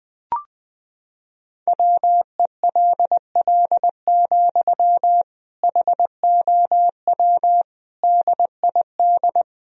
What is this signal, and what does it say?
Morse code, heard as a keyed tone, about 20 wpm, 700 Hz. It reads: WELL, HOW DID
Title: Sentences from Top 1000 Words - Review Part 2 - 20wpm